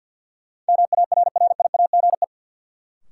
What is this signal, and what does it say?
Morse code, heard as a keyed tone, about 53 wpm, 700 Hz. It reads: MARRIAGE
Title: Sentences from Top 800 Words - New Word Drill - 50wpm